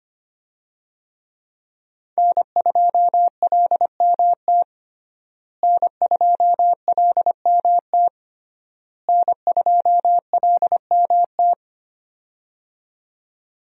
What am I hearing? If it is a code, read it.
Morse code, heard as a keyed tone, about 25 wpm, 700 Hz. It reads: N2LMT N2LMT N2LMT
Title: USA Call Signs - 5 Characters - Repeated 3x - 25wpm